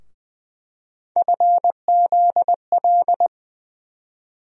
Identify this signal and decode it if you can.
Morse code, heard as a keyed tone, about 20 wpm, 700 Hz. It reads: FZL
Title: USA Call Sign Postfixes - 3 Letters 20wpm